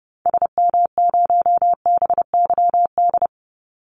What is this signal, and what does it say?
Morse code, heard as a keyed tone, about 30 wpm, 700 Hz. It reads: SM0BYD
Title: Call Signs - 30wpm